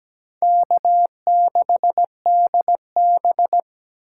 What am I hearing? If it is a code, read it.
Morse code, heard as a keyed tone, about 17 wpm, 700 Hz. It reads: K6DB
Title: USA Call Signs - 8x Character Spacing - 17wpm